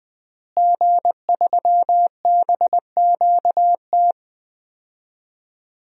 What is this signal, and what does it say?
Morse code, heard as a keyed tone, about 19 wpm, 700 Hz. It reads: G3BQT
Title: DXCC-100 Call Signs - 5 Characters - Repeated 3x - 20wpm